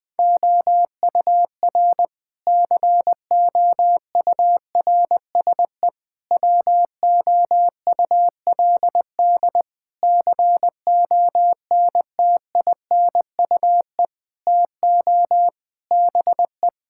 Morse code, heard as a keyed tone, about 20 wpm, 700 Hz. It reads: OUR COURSE WOULD CONTINUE TO BE
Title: Sentences from Top 1000 Words - Review Part 6 - 20wpm